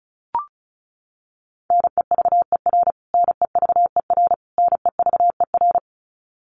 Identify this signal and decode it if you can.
Morse code, heard as a keyed tone, about 34 wpm, 700 Hz. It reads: NEVER NEVER NEVER
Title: Top 500 Words Repeated 35wpm